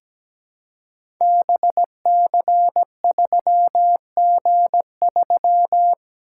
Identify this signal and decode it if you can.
Morse code, heard as a keyed tone, about 17 wpm, 700 Hz. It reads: BC3G3
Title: Code Groups - 5 Letter:Numbers - 17wpm